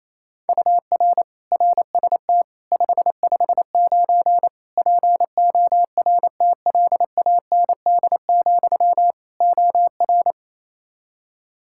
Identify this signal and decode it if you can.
Morse code, heard as a keyed tone, about 28 wpm, 700 Hz. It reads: UR RST 559 PORTLAND, OR
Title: K1USN SST QSO Practice 28wpm